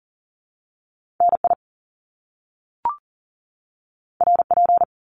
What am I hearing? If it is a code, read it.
Morse code, heard as a keyed tone, about 40 wpm, 700 Hz. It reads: NI RP
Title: Top Two Letter Combinations in Words 40wpm